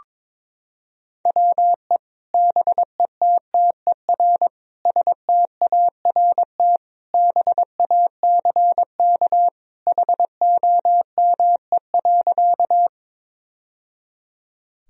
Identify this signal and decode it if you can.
Morse code, heard as a keyed tone, about 22 wpm, 700 Hz. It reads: WE BETTER START BACK HOME.
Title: Sentences from Top 1000 Words - Review Part 1 - 22wpm